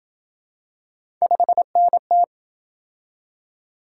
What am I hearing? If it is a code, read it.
Morse code, heard as a keyed tone, about 27 wpm, 700 Hz. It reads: 5NT